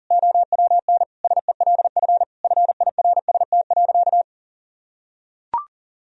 Morse code, heard as a keyed tone, about 37 wpm, 700 Hz. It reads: OWN SELF FIRST.
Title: Sentences from Top 1000 Words - Review Part 1 - 40wpm